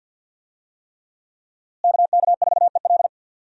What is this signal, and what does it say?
Morse code, heard as a keyed tone, about 52 wpm, 700 Hz. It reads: KK4EL
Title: USA Call Signs - 5 Characters - Repeated 3x - 50wpm